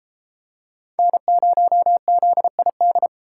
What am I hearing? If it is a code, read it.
Morse code, heard as a keyed tone, about 31 wpm, 700 Hz. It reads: N0ZID